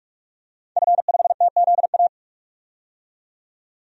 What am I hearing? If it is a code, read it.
Morse code, heard as a keyed tone, about 46 wpm, 700 Hz. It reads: F5TZA
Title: DXCC-100 Call Signs - 5 Characters - 45wpm